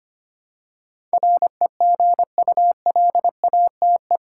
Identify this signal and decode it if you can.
Morse code, heard as a keyed tone, about 25 wpm, 700 Hz. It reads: REGULATE
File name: 8 Letter Words - First 3 Letters Given - 25wpm